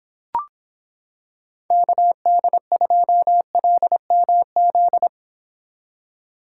Morse code, heard as a keyed tone, about 26 wpm, 700 Hz. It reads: KD2LMZ